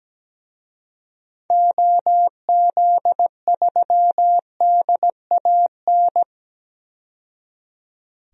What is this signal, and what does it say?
Morse code, heard as a keyed tone, about 17 wpm, 700 Hz. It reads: OZ3DAN
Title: DXCC-100 Call Signs - 6 Characters - 17wpm